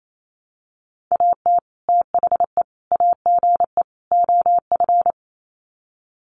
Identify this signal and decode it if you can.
Morse code, heard as a keyed tone, about 28 wpm, 700 Hz. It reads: AT THE AGE OF
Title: Sets of 4 Words 28wpm